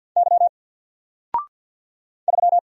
Morse code, heard as a keyed tone, about 50 wpm, 700 Hz. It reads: Y 3
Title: Single Letter-Number 50wpm